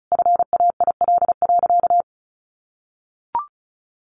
Morse code, heard as a keyed tone, about 34 wpm, 700 Hz. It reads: FAIL.
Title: Sentences from Top 1500 Words 35wpm